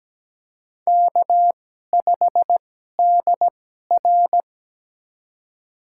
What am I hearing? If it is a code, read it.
Morse code, heard as a keyed tone, about 17 wpm, spaced as for 12 wpm, 700 Hz. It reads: K5DR
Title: USA Call Signs - 4 Characters - 2x Character Spacing - 17wpm